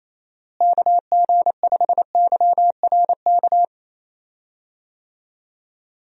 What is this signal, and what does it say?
Morse code, heard as a keyed tone, about 28 wpm, 700 Hz. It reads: KG5YRK
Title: USA Call Signs - 6 Characters - Repeated 3x - 28wpm